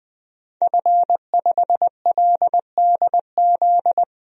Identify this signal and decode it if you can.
Morse code, heard as a keyed tone, about 20 wpm, 700 Hz. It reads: F5LDZ
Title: DXCC-100 Call Signs - 5 Characters - 20wpm